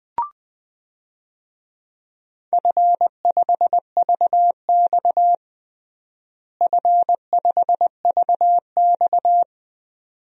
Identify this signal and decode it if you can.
Morse code, heard as a keyed tone, about 20 wpm, 700 Hz. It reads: F5VX F5VX
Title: DXCC-100 Call Signs - 4 Characters - Repeated 2x - 20wpm